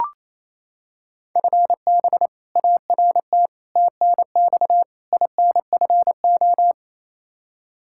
Morse code, heard as a keyed tone, about 28 wpm, 700 Hz. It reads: FB ART TNX INFO
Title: K1USN SST QSO Practice 28wpm